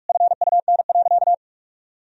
Morse code, heard as a keyed tone, about 48 wpm, 700 Hz. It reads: FUN.
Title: Sentences from Top 1000 Words - Review Part 4 - 45wpm